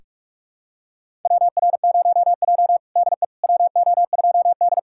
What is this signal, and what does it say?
Morse code, heard as a keyed tone, about 47 wpm, 700 Hz. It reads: WR0J DE WO2D
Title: USA Callsign DE Callsign - 4 Characters - 45wpm